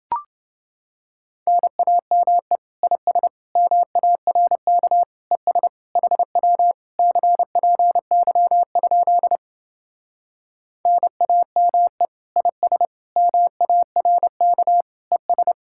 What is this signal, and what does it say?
Morse code, heard as a keyed tone, about 28 wpm, 700 Hz. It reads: NAME IS MARK ES HW CPY? NAME IS MARK ES